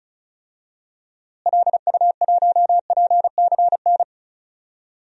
Morse code, heard as a keyed tone, about 36 wpm, 700 Hz. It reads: LU1PCN
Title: DXCC-100 Call Signs - 6 Characters - Repeated 3x - 35wpm